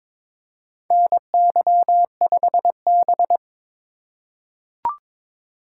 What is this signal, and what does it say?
Morse code, heard as a keyed tone, about 22 wpm, 700 Hz. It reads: NY5B
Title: USA Call Signs - 4x Character Spacing - 22wpm